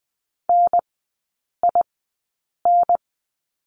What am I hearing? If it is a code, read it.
Morse code, heard as a keyed tone, about 20 wpm, spaced as for 8 wpm, 700 Hz. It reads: NIN
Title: 152 Lesson 25e - Letter p - Three Characters 20wpm